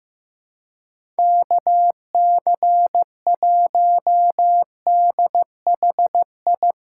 Morse code, heard as a keyed tone, about 15 wpm, 700 Hz. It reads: KC1DHI
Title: USA Callsign DE Callsign - 6 Characters - 15wpm